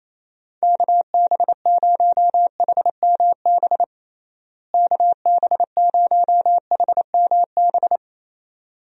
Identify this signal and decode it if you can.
Morse code, heard as a keyed tone, about 28 wpm, 700 Hz. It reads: KB0HMB KB0HMB
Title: USA Call Signs - 6 Characters - Repeated 2x - 28wpm